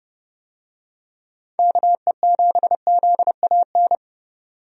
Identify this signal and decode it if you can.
Morse code, heard as a keyed tone, about 30 wpm, 700 Hz. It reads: KE7ZAN
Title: USA Call Signs - 6 Characters - 30wpm